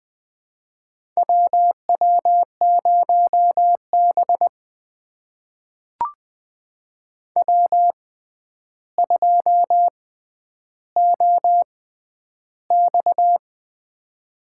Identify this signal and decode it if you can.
Morse code, heard as a keyed tone, about 20 wpm, spaced as for 10 wpm, 700 Hz. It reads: WW0B W2OX
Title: USA Call Signs - 4 Characters - 6x Character Spacing - 20wpm